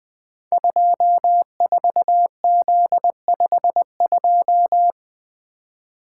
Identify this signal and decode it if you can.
Morse code, heard as a keyed tone, about 20 wpm, 700 Hz. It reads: 24Z52
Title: Code Groups - 5 Letter:Numbers - 20wpm